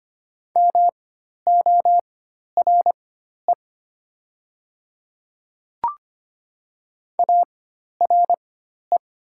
Morse code, heard as a keyed tone, about 25 wpm, spaced as for 12 wpm, 700 Hz. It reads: MORE ARE
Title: 4x Character Spacing - Top 100 Words - 25wpm